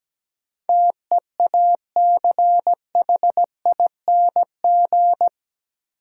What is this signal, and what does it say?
Morse code, heard as a keyed tone, about 17 wpm, 700 Hz. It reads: TEACHING
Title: Sentences from Top 1100 Words - New Word Drill - 17wpm